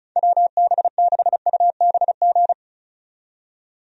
Morse code, heard as a keyed tone, about 36 wpm, 700 Hz. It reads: WB6UBG
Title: USA Call Signs - 6 Characters - 35wpm